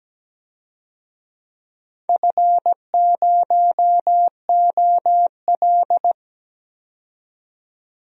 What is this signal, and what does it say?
Morse code, heard as a keyed tone, about 17 wpm, 700 Hz. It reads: F0OL
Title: DXCC-100 Call Signs - 4 Characters - 17wpm